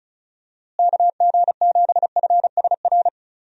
Morse code, heard as a keyed tone, about 34 wpm, 700 Hz. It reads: KG7FSR
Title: USA Call Signs - 6 Characters - Repeated 2x - 35wpm